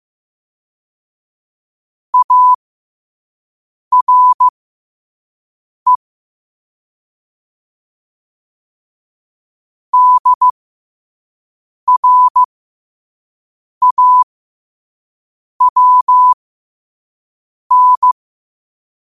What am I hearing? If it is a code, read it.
Morse code, heard as a keyed tone, about 15 wpm, spaced as for 5 wpm, 1000 Hz. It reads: ARE DRAWN
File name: qotdfiles / 5wpm